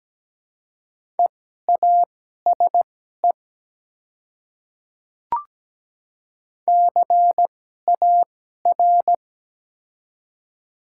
Morse code, heard as a keyed tone, about 17 wpm, spaced as for 12 wpm, 700 Hz. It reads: EASE CAR
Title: 2x Character Spacing - Top 300 Words - 17wpm